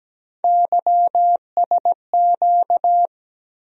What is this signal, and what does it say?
Morse code, heard as a keyed tone, about 17 wpm, 700 Hz. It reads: YSQ
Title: USA Call Sign Postfixes - 3 Letters 17wpm